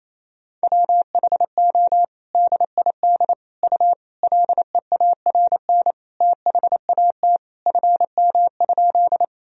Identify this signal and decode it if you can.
Morse code, heard as a keyed tone, about 27 wpm, 700 Hz. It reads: WHO DID U LEARN THAT FM?
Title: Sentences from Top 300 Words and Abbreviations 28wpm